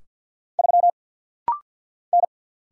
Morse code, heard as a keyed tone, about 49 wpm, 690 Hz. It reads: 3 N
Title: Single Letter-Number - Rapid-Fire 50wpm